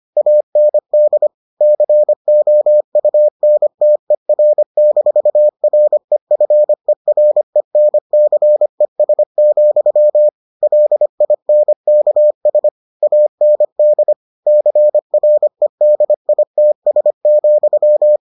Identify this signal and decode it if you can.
Morse code, heard as a keyed tone, about 25 wpm, 600 Hz. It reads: AND COUNTER-REFERENCES, LINKS AND CREDITS,